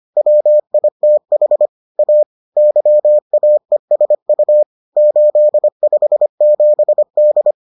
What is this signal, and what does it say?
Morse code, heard as a keyed tone, about 25 wpm, 600 Hz. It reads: WITH A YAESU 857D